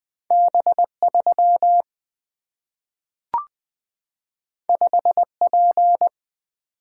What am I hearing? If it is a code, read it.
Morse code, heard as a keyed tone, about 20 wpm, 700 Hz. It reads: B3 5P
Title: Single-Letter Single-Number 20wpm